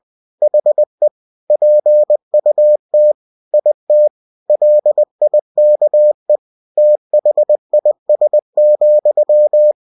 Morse code, heard as a keyed tone, about 20 wpm, 600 Hz. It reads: HE PUT IT LIKE THIS,